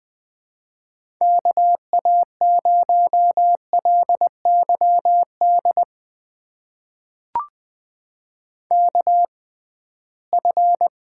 Morse code, heard as a keyed tone, about 20 wpm, spaced as for 15 wpm, 700 Hz. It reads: KA0LYD KF
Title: USA Call Signs - 6 Characters - 6x Character Spacing - 20wpm